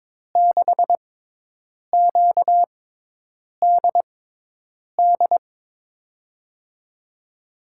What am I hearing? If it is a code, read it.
Morse code, heard as a keyed tone, about 21 wpm, spaced as for 8 wpm, 700 Hz. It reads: 6QDD
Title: USA Call Signs - 6 Characters - 6x Character Spacing - 22wpm